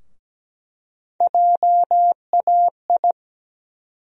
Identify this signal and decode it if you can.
Morse code, heard as a keyed tone, about 17 wpm, 700 Hz. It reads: JAI
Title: USA Call Sign Postfixes - 3 Letters 17wpm